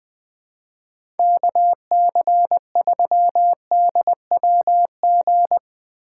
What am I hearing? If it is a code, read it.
Morse code, heard as a keyed tone, about 20 wpm, 700 Hz. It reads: KC3DWG
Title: USA Call Signs - 6 Characters - 4x Character Spacing - 20wpm